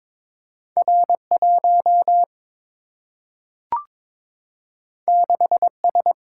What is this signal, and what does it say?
Morse code, heard as a keyed tone, about 22 wpm, 700 Hz. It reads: R1 6S
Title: Single-Letter Single-Number 22wpm